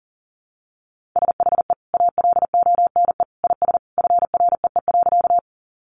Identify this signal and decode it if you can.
Morse code, heard as a keyed tone, about 37 wpm, 700 Hz. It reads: SHE ALONE IS FREE.
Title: Sentences from Top 1000 Words 40wpm